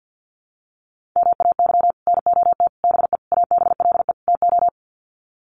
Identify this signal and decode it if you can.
Morse code, heard as a keyed tone, about 50 wpm, 700 Hz. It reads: MAY NOT BE ABLE TO
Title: Sets of 5 Words 50wpm